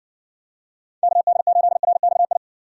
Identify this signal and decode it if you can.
Morse code, heard as a keyed tone, about 60 wpm, 700 Hz. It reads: KD8RXI